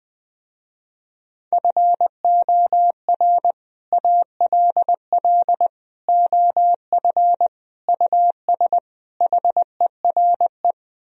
Morse code, heard as a keyed tone, about 20 wpm, 700 Hz. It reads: FOR ALL OF US HERE
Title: Speed-Racing - Sets of 5 Words 20wpm